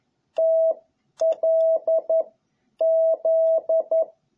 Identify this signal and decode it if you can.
Morse code, heard as a keyed tone, about 11 wpm, 660 Hz. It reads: T L Z
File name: jh8cbh / 2021cw